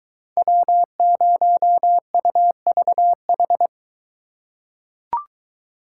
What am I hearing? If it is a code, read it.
Morse code, heard as a keyed tone, about 21 wpm, 700 Hz. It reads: W0UVH